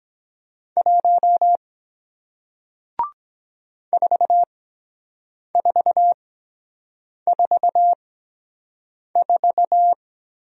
Morse code, heard as a keyed tone, about 23 wpm, spaced as for 7 wpm, 700 Hz. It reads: 1 4444